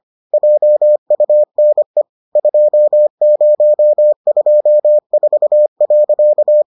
Morse code, heard as a keyed tone, about 25 wpm, 600 Hz. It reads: JUNE 2024.